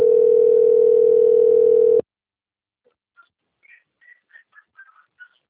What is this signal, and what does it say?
Ringback tone (North America), 440 + 480 Hz.